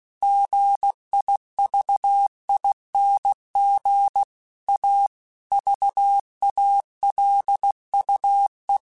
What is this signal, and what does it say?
Morse code, heard as a keyed tone, about 16 wpm, 800 Hz. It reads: GIVING A VALUE
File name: Technical Subjects / Technical Subjects-mp3s / XL @15